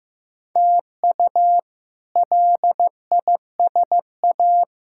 Morse code, heard as a keyed tone, about 15 wpm, 700 Hz. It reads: TU LISA
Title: K1USN SST QSO Practice - 15wpm